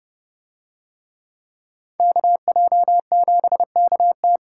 Morse code, heard as a keyed tone, about 30 wpm, 700 Hz. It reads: KJ7KT